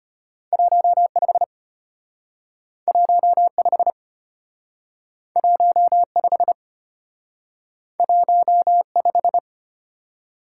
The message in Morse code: 15 15 15 15

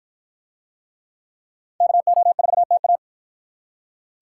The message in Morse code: KO4TA

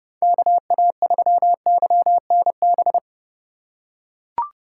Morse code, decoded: KA3YNB

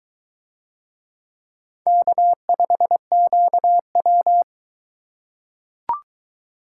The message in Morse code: K5QW

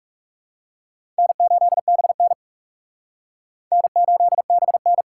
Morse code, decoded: N8BN N8BN

